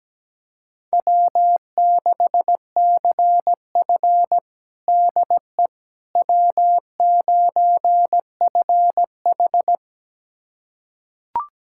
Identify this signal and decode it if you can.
Morse code, heard as a keyed tone, about 17 wpm, 700 Hz. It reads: W6CF DE W9FH
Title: USA Callsign DE Callsign - 4 Characters - 17wpm